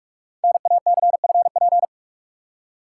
Morse code, heard as a keyed tone, about 44 wpm, 700 Hz. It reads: NACFP
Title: Code Groups - 5 Letter:Numbers - 45wpm